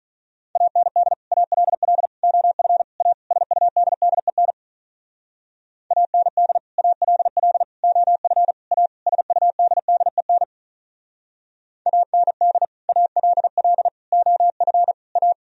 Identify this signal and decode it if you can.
Morse code, heard as a keyed tone, about 43 wpm, 700 Hz. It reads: AND ALL OF A SUDDEN AND ALL OF A SUDDEN AND ALL OF A
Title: Speed-Racing - Sets of 5 Words 35wpm